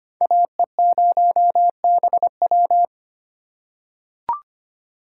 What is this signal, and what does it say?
Morse code, heard as a keyed tone, about 25 wpm, 700 Hz. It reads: AE0BW
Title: USA Call Signs - 5 Characters - 4x Character Spacing - 25wpm